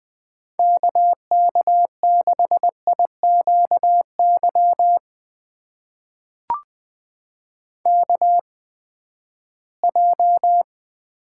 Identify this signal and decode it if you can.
Morse code, heard as a keyed tone, about 20 wpm, spaced as for 14 wpm, 700 Hz. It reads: KK6IQY KJ
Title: USA Call Signs - 6 Characters - 8x Character Spacing - 20wpm